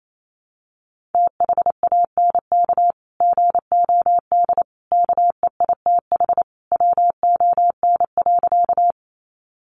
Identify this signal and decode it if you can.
Morse code, heard as a keyed tone, about 28 wpm, 700 Hz. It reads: THANK GOD KEITH WON.
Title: A Sound Of Thunder 28wpm